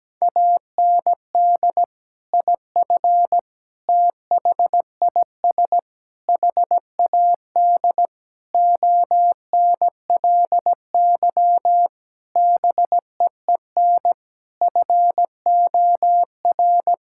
Morse code, heard as a keyed tone, about 17 wpm, 700 Hz. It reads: AND IF THIS HAD ONLY BEEN FOR
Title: Sentences from Top 1600 Words 17wpm